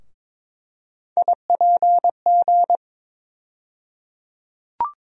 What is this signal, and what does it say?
Morse code, heard as a keyed tone, about 22 wpm, 700 Hz. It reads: IPG
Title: USA Call Sign Postfixes - 3 Letters 22wpm